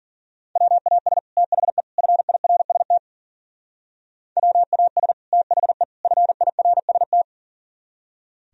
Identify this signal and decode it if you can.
Morse code, heard as a keyed tone, about 52 wpm, 700 Hz. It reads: WAS THE FIRST WAS THE FIRST